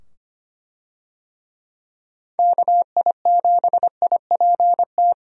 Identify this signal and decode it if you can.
Morse code, heard as a keyed tone, about 25 wpm, 700 Hz. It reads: KI7IPT
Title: USA Call Signs - 6 Characters - 25wpm